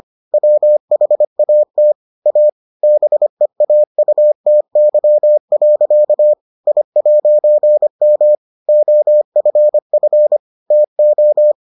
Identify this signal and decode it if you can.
Morse code, heard as a keyed tone, about 25 wpm, 600 Hz. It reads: WHAT A BEAUTY. I'M OFF TO